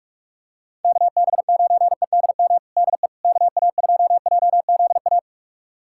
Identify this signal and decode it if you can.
Morse code, heard as a keyed tone, about 41 wpm, 700 Hz. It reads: KB9EDM DE KA2JZA
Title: USA Callsign DE Callsign 45wpm